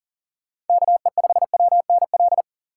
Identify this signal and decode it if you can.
Morse code, heard as a keyed tone, about 50 wpm, 700 Hz. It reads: KE5WNL
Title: Speed Racing - USA Call Signs 40wpm